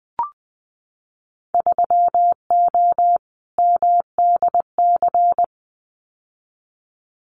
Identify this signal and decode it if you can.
Morse code, heard as a keyed tone, about 19 wpm, 700 Hz. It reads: 3O MDC
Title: Winter Field Day Exchanges 20wpm